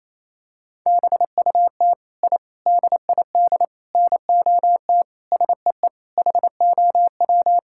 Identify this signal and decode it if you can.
Morse code, heard as a keyed tone, about 28 wpm, 700 Hz. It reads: BUT I DID NOT SEE HOW
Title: Sentences from Top 1000 Words - Review Part 5 - 28wpm